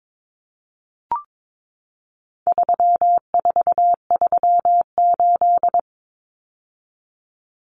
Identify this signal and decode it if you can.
Morse code, heard as a keyed tone, about 22 wpm, 700 Hz. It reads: 3438
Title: Numbers - 4-Digit 22wpm